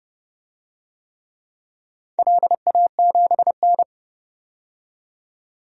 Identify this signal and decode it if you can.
Morse code, heard as a keyed tone, about 28 wpm, 700 Hz. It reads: LA7N
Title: DXCC-100 Call Signs - 4 Characters - 30wpm